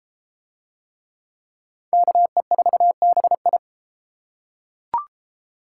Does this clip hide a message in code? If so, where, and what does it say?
Yes: Morse code from 1.9 to 3.6 s, heard as a keyed tone, about 30 wpm, 700 Hz. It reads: KE4BI